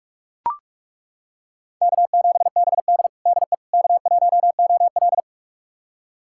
Morse code, heard as a keyed tone, about 41 wpm, 700 Hz. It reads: K7BD DE K1OL
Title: USA Callsign DE Callsign - 4 Characters - 45wpm